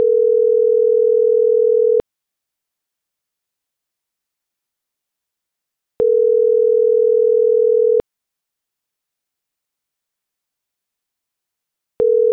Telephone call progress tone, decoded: ringback tone

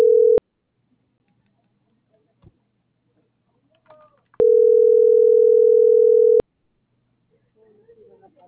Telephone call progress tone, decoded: ringback tone

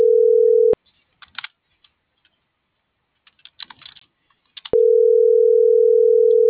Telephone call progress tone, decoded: ringback tone

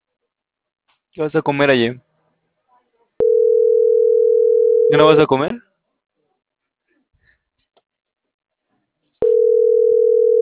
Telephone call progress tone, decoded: ringback tone

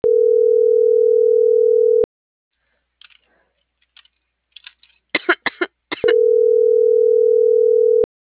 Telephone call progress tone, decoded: ringback tone

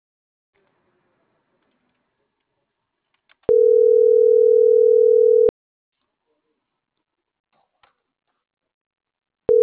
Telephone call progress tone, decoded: ringback tone

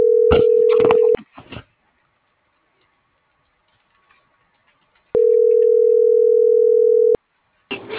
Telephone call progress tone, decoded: ringback tone

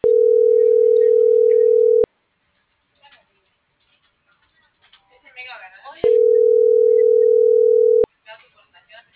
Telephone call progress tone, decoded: ringback tone

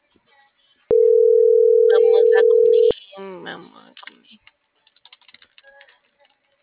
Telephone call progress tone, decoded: ringback tone

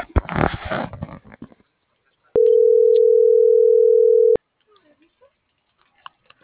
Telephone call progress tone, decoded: ringback tone

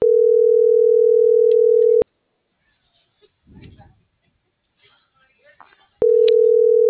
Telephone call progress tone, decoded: ringback tone

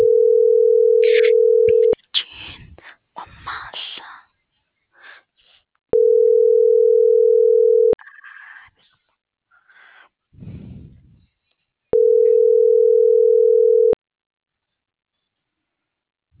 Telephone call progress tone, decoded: ringback tone

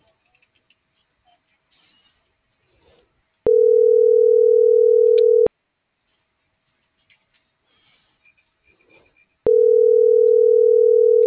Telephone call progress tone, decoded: ringback tone

